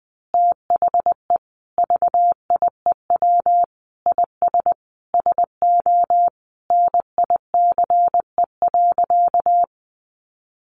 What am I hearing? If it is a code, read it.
Morse code, heard as a keyed tone, about 20 wpm, 700 Hz. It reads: THE VIEW IS SO NICE.
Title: Additional Easy Phrases 20wpm